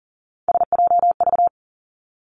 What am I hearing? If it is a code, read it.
Morse code, heard as a keyed tone, about 40 wpm, 700 Hz. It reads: SJV
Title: ARRL Sections 40wpm